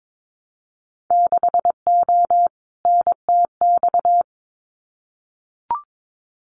Morse code, heard as a keyed tone, about 22 wpm, 700 Hz. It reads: 6O NTX
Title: Winter Field Day Exchanges Encore 22wpm